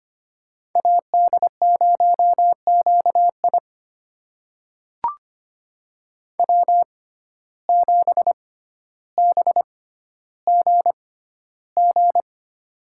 Morse code, heard as a keyed tone, about 25 wpm, spaced as for 13 wpm, 700 Hz. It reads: AD0QI W7BGG
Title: USA Call Signs - 5 Characters - 6x Character Spacing - 25wpm